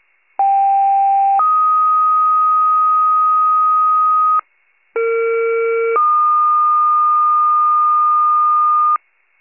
Two-tone page, heard as a one-second tone, then a three-second tone, about 790 then 1250 Hz, about 460 then 1220 Hz. It is heard from 0.4 to 9.0 s.